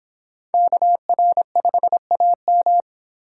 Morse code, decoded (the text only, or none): KR5AM